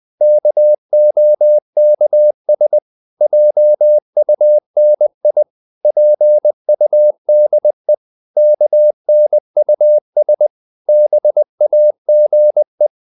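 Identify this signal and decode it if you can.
Morse code, heard as a keyed tone, about 20 wpm, 600 Hz. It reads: KOKS JUNI PUDE KNUS BAGE
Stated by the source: Morse code practice of short Danish words at 20 wpm, from CW med Gnister.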